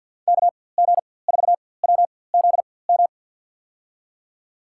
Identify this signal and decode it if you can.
Morse code, heard as a keyed tone, about 53 wpm, spaced as for 24 wpm, 700 Hz. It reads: KG4WZM
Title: USA Call Signs - 6 Characters - 4x Character Spacing - 50wpm